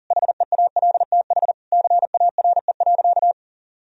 Morse code, heard as a keyed tone, about 40 wpm, 700 Hz. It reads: HEALTH CARE.
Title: Sentences from Top 1000 Words - Review Part 4 - 40wpm